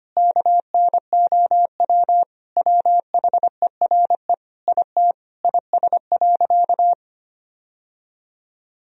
Morse code, heard as a keyed tone, about 25 wpm, 700 Hz. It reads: KNOW WHERE IT IS.
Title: Sentences from Top 1000 Words - Review Part 4 - 25wpm